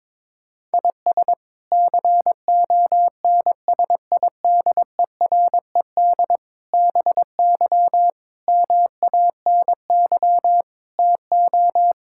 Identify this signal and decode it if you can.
Morse code, heard as a keyed tone, about 22 wpm, 700 Hz. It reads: IS CONSIDERED BY MANY TO
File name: practice / 22wpm / Sets of 7 Words - 22wpm